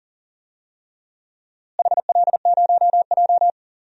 Morse code, heard as a keyed tone, about 40 wpm, 700 Hz. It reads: HL0J
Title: DXCC-100 Call Signs - 4 Characters - Repeated 3x - 40wpm